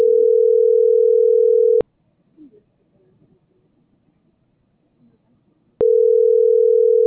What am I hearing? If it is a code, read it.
Ringback tone (North America), 440 + 480 Hz.